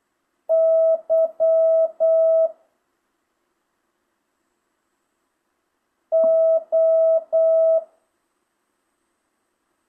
Morse code, heard as a keyed tone, about 8 wpm, 650 Hz. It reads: Y O
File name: jh8cbh / 2025cw